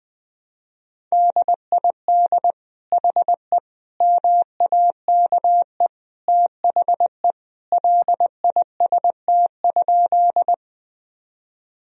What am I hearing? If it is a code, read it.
Morse code, heard as a keyed tone, about 19 wpm, 700 Hz. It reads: DID HE MAKE THE LIST?
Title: Sentences from Top 1000 Words - Review Part 2 - 20wpm